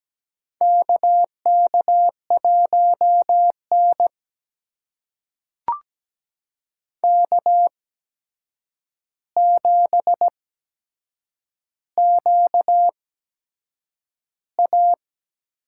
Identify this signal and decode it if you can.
Morse code, heard as a keyed tone, about 17 wpm, spaced as for 7 wpm, 700 Hz. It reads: KK1N K7QA
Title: USA Call Signs - 4 Characters - 8x Character Spacing - 17wpm